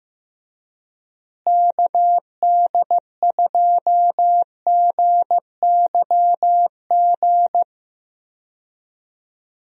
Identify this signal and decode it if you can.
Morse code, heard as a keyed tone, about 16 wpm, 700 Hz. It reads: KD2GYG
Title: USA Call Signs - 6 Characters - Repeated 2x - 15wpm